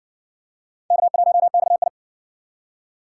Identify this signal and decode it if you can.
Morse code, heard as a keyed tone, about 60 wpm, 700 Hz. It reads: K1XI